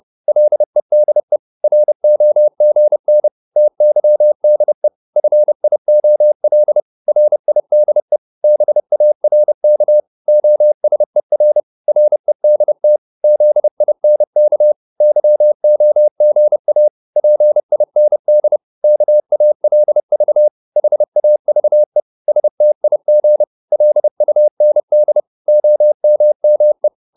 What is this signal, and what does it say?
Morse code, heard as a keyed tone, about 30 wpm, 600 Hz. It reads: LEDE ROGN TYDE FIOL RIDE BARK OSER REDT ZINK YOGA PIND KALV HAVE STIG LUND OMME
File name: Korte ord DK 30wpm